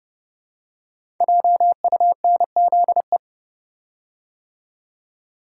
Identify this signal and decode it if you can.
Morse code, heard as a keyed tone, about 30 wpm, 700 Hz. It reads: JUNZE
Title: Code Groups - 5 Letters - 30wpm